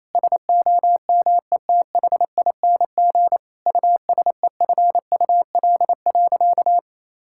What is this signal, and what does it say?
Morse code, heard as a keyed tone, about 28 wpm, 700 Hz. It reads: SOMETHING USEFUL.
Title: Sentences from Top 1000 Words - Review Part 5 - 28wpm